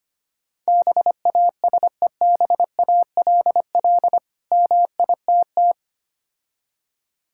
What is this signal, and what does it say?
Morse code, heard as a keyed tone, about 25 wpm, 690 Hz. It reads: BASEBALL MITT
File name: practice / 25wpm / Difficult Words with Hints - 25wpm